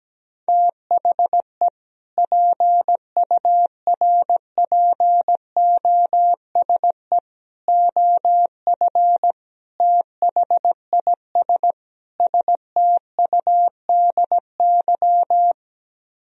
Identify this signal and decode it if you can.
Morse code, heard as a keyed tone, about 17 wpm, 700 Hz. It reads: THE PURPOSE OF THIS STUDY
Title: Speed-Racing - Sets of 5 Words 17wpm